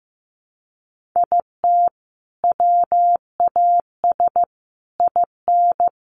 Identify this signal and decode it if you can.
Morse code, heard as a keyed tone, about 15 wpm, 700 Hz. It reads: IT WAS IN
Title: Sentences from Top 1400 Words 15wpm